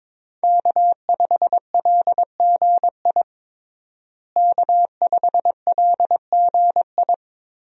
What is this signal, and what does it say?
Morse code, heard as a keyed tone, about 22 wpm, 700 Hz. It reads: K5LGI K5LGI
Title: USA Call Signs - 5 Characters - Repeated 3x - 22wpm